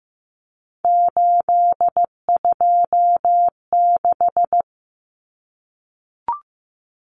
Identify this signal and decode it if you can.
Morse code, heard as a keyed tone, about 15 wpm, 700 Hz. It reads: 826
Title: Common Cut Numbers - 3-Digit 15wpm